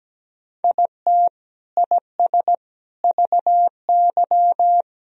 Morse code, heard as a keyed tone, about 17 wpm, 700 Hz. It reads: IT IS VY
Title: Sentences from Top 200 Words and Abbreviations 17wpm